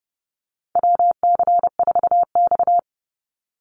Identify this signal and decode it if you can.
Morse code, heard as a keyed tone, about 32 wpm, 700 Hz. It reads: WC4X
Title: Call Signs - 30wpm